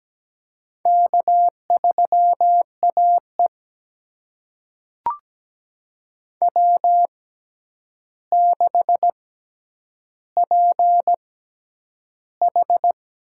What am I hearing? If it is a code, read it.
Morse code, heard as a keyed tone, about 17 wpm, spaced as for 9 wpm, 700 Hz. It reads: K3AE W6PH
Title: USA Call Signs - 4 Characters - 6x Character Spacing - 17wpm